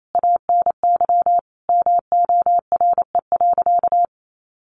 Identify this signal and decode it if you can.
Morse code, heard as a keyed tone, about 28 wpm, 700 Hz. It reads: ANY MORE.
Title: Sentences from Top 1300 Words 28wpm